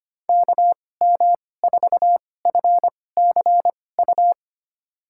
Morse code, heard as a keyed tone, about 25 wpm, spaced as for 18 wpm, 700 Hz. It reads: KM4FCU
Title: USA Call Signs - 6 Characters - 2x Character Spacing - 25wpm